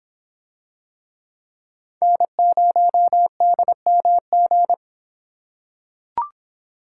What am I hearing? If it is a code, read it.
Morse code, heard as a keyed tone, about 26 wpm, 700 Hz. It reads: N0DMG